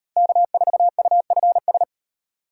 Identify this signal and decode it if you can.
Morse code, heard as a keyed tone, about 38 wpm, 700 Hz. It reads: K4UFS